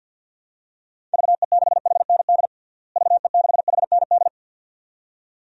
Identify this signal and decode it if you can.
Morse code, heard as a keyed tone, about 51 wpm, 700 Hz. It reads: VE6HND VE6HND
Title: DXCC-100 Call Signs - 6 Characters - Repeated 2x - 50wpm